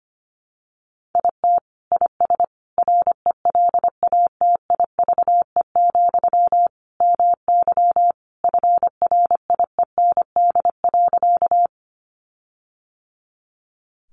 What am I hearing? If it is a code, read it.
Morse code, heard as a keyed tone, about 25 wpm, 700 Hz. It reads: IT IS RELATIVE, MY FRIEND.
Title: Sentences from Top 1400 Words 25wpm